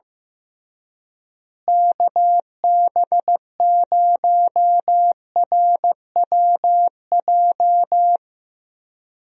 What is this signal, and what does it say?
Morse code, heard as a keyed tone, about 15 wpm, 700 Hz. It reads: KB0RWJ
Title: USA Call Signs - 6 Characters - Repeated 2x - 15wpm